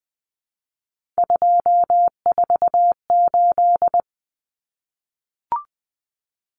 Morse code, heard as a keyed tone, about 20 wpm, 700 Hz. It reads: 248
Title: Numbers - 3-Digit 20wpm